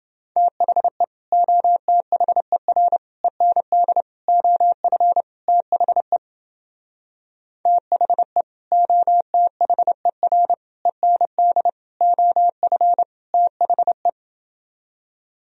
Morse code, heard as a keyed tone, about 31 wpm, 700 Hz. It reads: THE OTHER END OF THE THE OTHER END OF THE